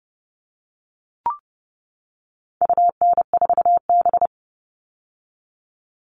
Morse code, heard as a keyed tone, about 30 wpm, 700 Hz. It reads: UN4B
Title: All Cut Numbers - 4-Digit 30wpm